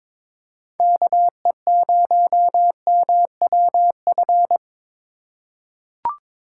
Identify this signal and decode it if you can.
Morse code, heard as a keyed tone, about 22 wpm, 700 Hz. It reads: KE0MWF
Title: USA Call Signs - 2x Character Spacing - 22wpm